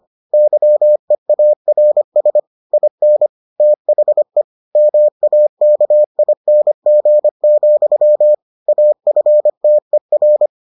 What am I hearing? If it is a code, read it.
Morse code, heard as a keyed tone, about 25 wpm, 600 Hz. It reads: YEARS IN THE MAKING, AFTER